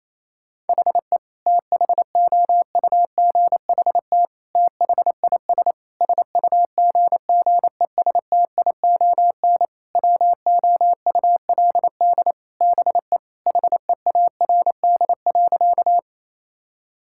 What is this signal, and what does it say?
Morse code, heard as a keyed tone, about 28 wpm, 700 Hz. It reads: HE THOUGHT THIS SUGGESTION WOULD BE HEARD.
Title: Sentences from Top 1600 Words 28wpm